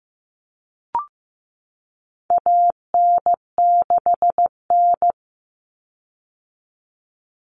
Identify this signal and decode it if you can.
Morse code, heard as a keyed tone, about 15 wpm, 700 Hz. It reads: AN6N
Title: Common Cut Numbers - 4-Digit 15wpm